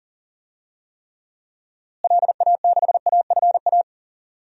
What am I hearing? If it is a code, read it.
Morse code, heard as a keyed tone, about 40 wpm, 700 Hz. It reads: LA6AFA
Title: DXCC-100 Call Signs - 6 Characters - Repeated 2x - 40wpm